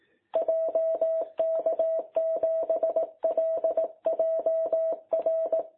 Morse code, heard as a keyed tone, about 18 wpm, 660 Hz. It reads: JK7LJR
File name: jh8cbh / 2021cw